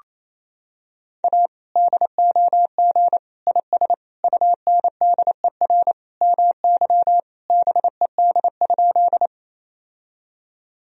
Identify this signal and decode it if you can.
Morse code, heard as a keyed tone, about 28 wpm, 700 Hz. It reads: A DOG IS UNDER MY BED?
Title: Sentences from Top 1000 Words - Review Part 2 - 28wpm